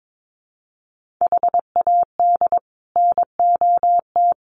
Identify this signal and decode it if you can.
Morse code, heard as a keyed tone, about 22 wpm, 700 Hz. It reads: HAD NOT
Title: Sets of 5 Words Encore 22wpm